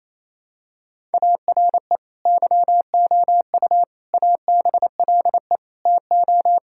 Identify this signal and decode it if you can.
Morse code, heard as a keyed tone, about 28 wpm, 700 Hz. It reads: ARE YOU ABLE TO
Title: Sentences from Top 1000 Words - Review Part 5 - 28wpm